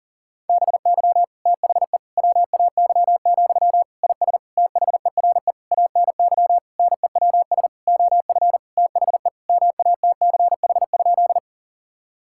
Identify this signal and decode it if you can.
Morse code, heard as a keyed tone, about 40 wpm, 700 Hz. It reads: BY THE WAY, IS THERE ANY NEWS OF THE MATCH?
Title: Sentences from Top 1600 Words 40wpm